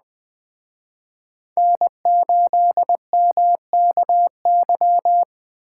Morse code, heard as a keyed tone, about 20 wpm, 700 Hz. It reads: N8MKY
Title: Speed Racing - USA 1x3 Call Signs 17wpm